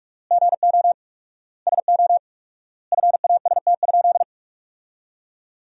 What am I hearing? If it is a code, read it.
Morse code, heard as a keyed tone, about 41 wpm, 700 Hz. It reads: GO SO FAST?
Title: 4x Word-Spacing - Sentences from Top 500 Words - 45wpm